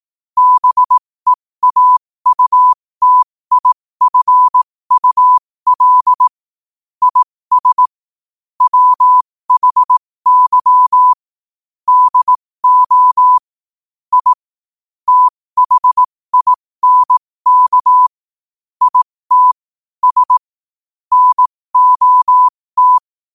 Morse code, heard as a keyed tone, about 18 wpm, spaced as for 15 wpm, 1000 Hz. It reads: BEAUTIFUL IS WHY DO I THINK IT S NOT
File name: qotdfiles / 15wpm